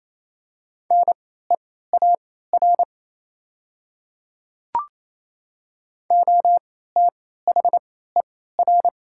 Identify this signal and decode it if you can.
Morse code, heard as a keyed tone, about 28 wpm, spaced as for 16 wpm, 700 Hz. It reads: NEAR OTHER
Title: 3x Character Spacing - Top 300 Words - 28wpm